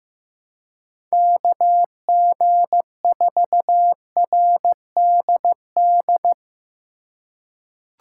Morse code, heard as a keyed tone, about 15 wpm, 700 Hz. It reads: KG4RDD
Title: USA Call Signs - 6 Characters - 2x Character Spacing - 15wpm